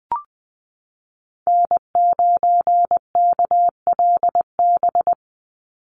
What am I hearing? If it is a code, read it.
Morse code, heard as a keyed tone, about 20 wpm, 700 Hz. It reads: N9KLB
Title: USA Call Signs 20wpm